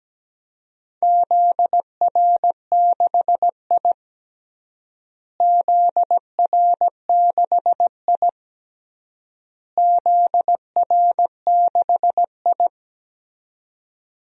Morse code, heard as a keyed tone, about 17 wpm, 700 Hz. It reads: ZR6I ZR6I ZR6I
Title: DXCC-100 Call Signs - 4 Characters - Repeated 3x - 17wpm